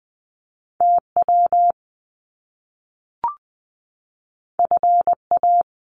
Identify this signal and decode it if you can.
Morse code, heard as a keyed tone, about 20 wpm, 700 Hz. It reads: TW FA
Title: Top Two Letter Combinations in Words 20wpm